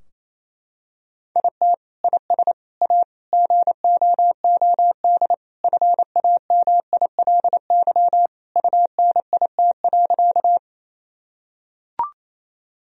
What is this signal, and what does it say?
Morse code, heard as a keyed tone, about 28 wpm, 700 Hz. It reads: IT IS A GOOD FAMILY UNIT.
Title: Sentences from Top 1000 Words - Review Part 2 - 28wpm